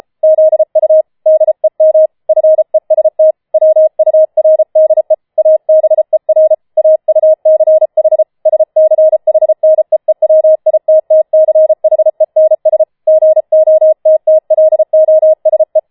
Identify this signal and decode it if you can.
Morse code, heard as a keyed tone, about 33 wpm, 630 Hz. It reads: ZU DEM FEST WURDE ABER AUCH SCHNEEWITTCHENS GOTTLOSE
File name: Schneewittchen Teil5 34wpm